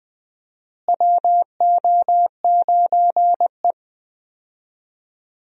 Morse code, heard as a keyed tone, about 20 wpm, 700 Hz. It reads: WO9E
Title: USA Call Signs - 4 Characters - Repeated 2x - 20wpm